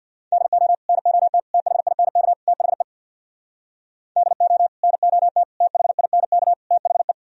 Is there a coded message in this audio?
Yes: Morse code from 0.3 to 7.1 s, heard as a keyed tone, about 64 wpm, 700 Hz. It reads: DO NOT THINK THE DO NOT THINK THE